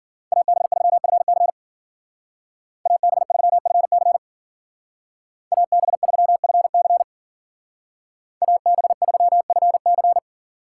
Morse code, heard as a keyed tone, about 49 wpm, 700 Hz. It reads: AB3FC AB3FC AB3FC AB3FC
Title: Speed Racing - USA 2x2 Call Signs 40wpm